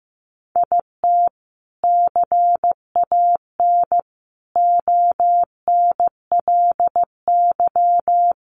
Morse code, heard as a keyed tone, about 15 wpm, 700 Hz. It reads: IT CAN ONLY
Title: Sentences from Top 1400 Words 15wpm